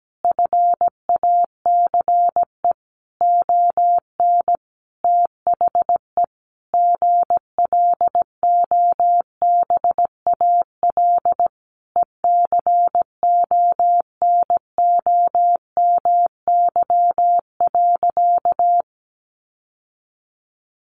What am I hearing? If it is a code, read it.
Morse code, heard as a keyed tone, about 17 wpm, 700 Hz. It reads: FACE ON THE GLOBAL ECONOMY.
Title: Sentences from Top 1200 Words 17wpm